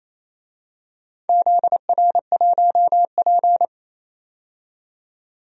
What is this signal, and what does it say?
Morse code, heard as a keyed tone, about 28 wpm, 700 Hz. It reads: ZR1P
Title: DXCC-100 Call Signs - 4 Characters - 28wpm